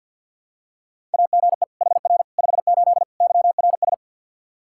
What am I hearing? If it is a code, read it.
Morse code, heard as a keyed tone, about 46 wpm, 700 Hz. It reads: AGE HR 58 YRS
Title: Practice QSO - Exchange 3 - More Info - 50wpm